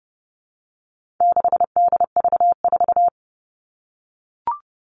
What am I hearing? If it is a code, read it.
Morse code, heard as a keyed tone, about 30 wpm, 700 Hz. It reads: 6DV4
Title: All Cut Numbers - 4-Digit 30wpm